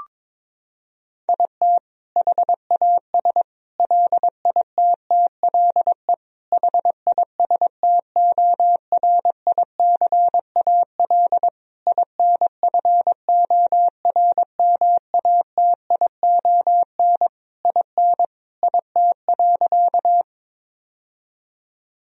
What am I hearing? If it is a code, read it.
Morse code, heard as a keyed tone, about 22 wpm, 700 Hz. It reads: IT HAS LITTLE HISTORICAL INFORMATION IN IT.
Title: Sentences from Top 1000 Words - Review Part 4 - 22wpm